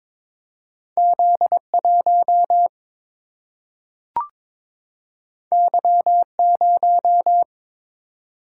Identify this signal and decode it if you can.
Morse code, heard as a keyed tone, about 22 wpm, 700 Hz. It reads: Z1 Y0
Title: Single-Letter Single-Number 22wpm